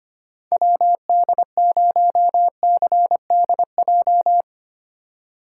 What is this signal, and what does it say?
Morse code, heard as a keyed tone, about 31 wpm, 700 Hz. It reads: WD0CDJ
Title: Speed Racing - USA Call Signs 25wpm